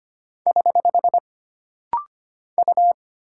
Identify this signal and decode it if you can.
Morse code, heard as a keyed tone, about 25 wpm, 700 Hz. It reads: <HH> U
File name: Single Character (Letter:Number:Punctuation:Prosign) - Rapid-Fire - 25wpm